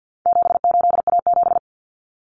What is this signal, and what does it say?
Morse code, heard as a keyed tone, about 50 wpm, 700 Hz. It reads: 78A7
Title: Common Cut Numbers - 4-Digit 50wpm